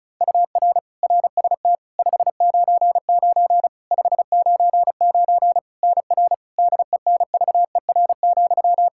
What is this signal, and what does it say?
Morse code, heard as a keyed tone, about 34 wpm, 700 Hz. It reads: UR RST 599 599 NR DENVER,
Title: Practice QSO - Exchange 1 - Beginning - 35wpm